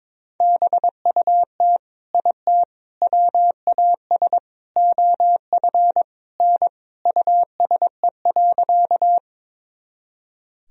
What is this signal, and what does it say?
Morse code, heard as a keyed tone, about 21 wpm, 700 Hz. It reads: BUT IT WAS OF N USE.
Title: Sentences from Top 100 Words and Abbreviations 22wpm